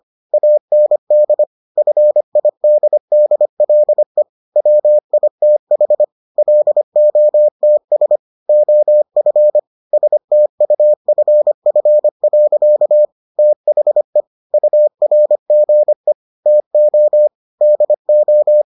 Morse code, heard as a keyed tone, about 25 wpm, 600 Hz. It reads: AND FIDDLE WITH LOTS OF STUFF. THE URGE TO DO